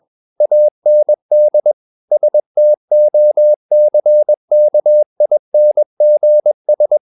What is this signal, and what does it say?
Morse code, heard as a keyed tone, about 21 wpm, 600 Hz. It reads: AND STOCKINGS
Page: CW med Gnister